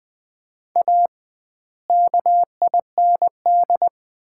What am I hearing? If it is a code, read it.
Morse code, heard as a keyed tone, about 20 wpm, 700 Hz. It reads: A KIND
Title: 2x Word-Spacing - Sets of 3 Words - 20wpm